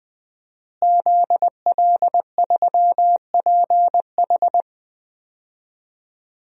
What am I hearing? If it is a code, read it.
Morse code, heard as a keyed tone, about 21 wpm, 700 Hz. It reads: ZL3PH
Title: DXCC-100 Call Signs - 5 Characters - Repeated 2x - 20wpm